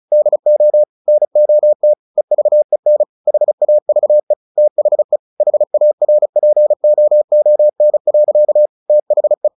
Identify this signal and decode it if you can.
Morse code, heard as a keyed tone, about 34 wpm, 600 Hz. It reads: DO NOT EVEN HAVE THE HARPOON. THE
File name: the old man and the sea 2 35wpm